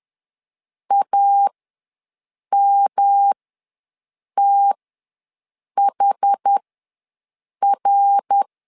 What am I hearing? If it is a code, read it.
Morse code, heard as a keyed tone, about 11 wpm, spaced as for 6 wpm, 800 Hz. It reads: AMTHR
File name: LEZIONI / 3 MWRO / 3 MWR ws 10wpm 2151